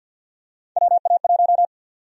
Morse code, heard as a keyed tone, about 50 wpm, 700 Hz. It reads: WA1
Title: USA Call Sign Prefixes - 50wpm